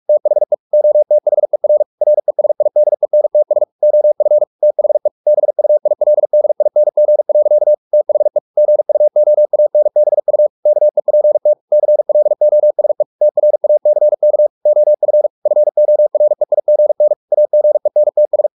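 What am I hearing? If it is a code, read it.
Morse code, heard as a keyed tone, about 44 wpm, 600 Hz. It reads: THE OTHER RESIDENTS OF THE BUILDING. THE GUOANBU KEPT CLOSE TRACK OF FOREIGN AGENTS
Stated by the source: Morse practice at 45 wpm, the novel Never by Ken Follett in English, from CW med Gnister.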